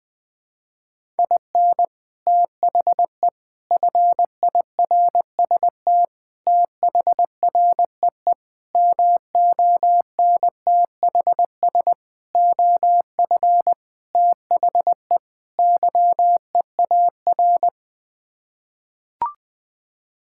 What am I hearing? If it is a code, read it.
Morse code, heard as a keyed tone, about 20 wpm, 700 Hz. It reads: IN THE FIRST THREE MONTHS OF THE YEAR
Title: Sets of 8 Words - 20wpm